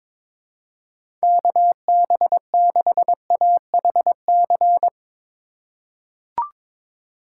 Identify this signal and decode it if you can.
Morse code, heard as a keyed tone, about 22 wpm, 700 Hz. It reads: KB6AHC
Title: USA Call Signs - 8x Character Spacing - 22wpm